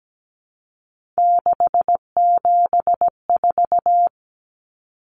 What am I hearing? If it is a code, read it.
Morse code, heard as a keyed tone, about 17 wpm, 700 Hz. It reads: 674
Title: Common Cut Numbers - 3-Digit 17wpm